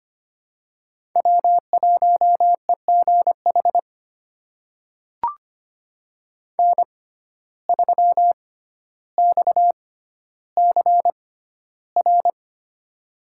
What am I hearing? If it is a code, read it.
Morse code, heard as a keyed tone, about 25 wpm, spaced as for 13 wpm, 700 Hz. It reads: W1EGH N3XCR
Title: USA Call Signs - 5 Characters - 6x Character Spacing - 25wpm